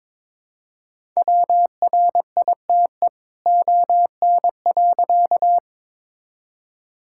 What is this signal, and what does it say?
Morse code, heard as a keyed tone, about 22 wpm, 700 Hz. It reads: WRITE ON.
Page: morse-code-ninja-practice-22wpm